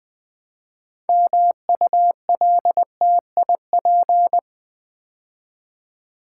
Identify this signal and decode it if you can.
Morse code, heard as a keyed tone, about 20 wpm, 700 Hz. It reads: MULTIP
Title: Build A Word - 3 Letters At A Time - 8-Letter Words - 20wpm